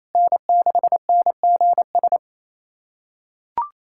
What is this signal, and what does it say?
Morse code, heard as a keyed tone, about 28 wpm, 700 Hz. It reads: N6NGS
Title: SOTA QSO Practice 28wpm